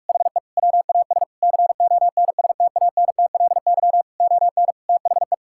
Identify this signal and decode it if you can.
Morse code, heard as a keyed tone, about 41 wpm, 700 Hz. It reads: HE WAS CONSTANTLY ON THE
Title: Sentences from Top 1600 Words 45wpm